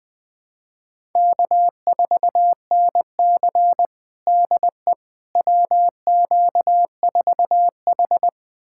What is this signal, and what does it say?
Morse code, heard as a keyed tone, about 20 wpm, 700 Hz. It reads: K4NC DE WQ4H
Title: USA Callsign DE Callsign - 4 Characters - 20wpm